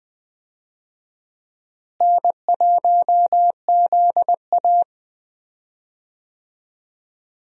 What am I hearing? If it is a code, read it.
Morse code, heard as a keyed tone, about 21 wpm, 700 Hz. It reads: N1ZA
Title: USA Call Signs - 4 Characters - Repeated 2x - 20wpm